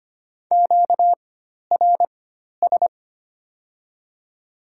Morse code, heard as a keyed tone, about 25 wpm, spaced as for 12 wpm, 700 Hz. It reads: QRS
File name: Top 90 QSO Elements - 4x Character Spacing - 25wpm